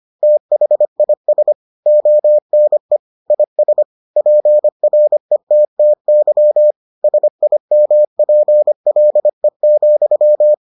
Morse code, heard as a keyed tone, about 25 wpm, 600 Hz. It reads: THIS ONE IS PRETTY SIMPLE,